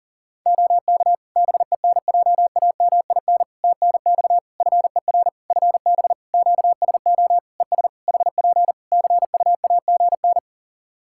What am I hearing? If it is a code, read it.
Morse code, heard as a keyed tone, about 40 wpm, 700 Hz. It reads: OK BENJAMIN TNX FER FB QSO ES HP CUAGN
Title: K1USN SST QSO Practice 40wpm